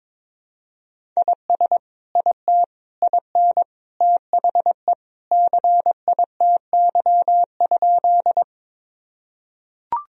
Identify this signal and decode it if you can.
Morse code, heard as a keyed tone, about 22 wpm, 700 Hz. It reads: IS IT IN THE CITY?
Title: Sentences from Top 1000 Words - Review Part 1 - 22wpm